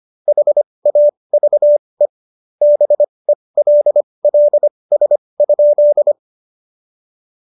Morse code, heard as a keyed tone, about 25 wpm, spaced as for 20 wpm, 590 Hz. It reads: HAVE BELLS?
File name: Jokes-25-char-20-eff